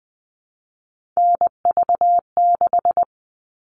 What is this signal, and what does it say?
Morse code, heard as a keyed tone, about 20 wpm, 700 Hz. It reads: NV6
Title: All Cut Numbers - 3-Digit 20wpm